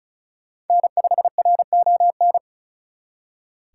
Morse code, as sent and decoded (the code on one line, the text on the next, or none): -. ..... .-. --- -.
N5RON